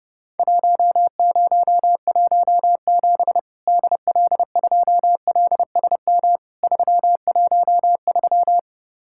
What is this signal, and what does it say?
Morse code, heard as a keyed tone, about 32 wpm, 700 Hz. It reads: 1017 DL2LSM 313
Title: Worked All Europe DX Contest - QTC Exchange - 30wpm